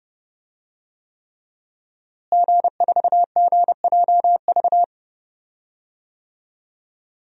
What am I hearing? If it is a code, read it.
Morse code, heard as a keyed tone, about 32 wpm, 700 Hz. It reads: G4GJV
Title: DXCC-100 Call Signs - 5 Characters - Repeated 3x - 30wpm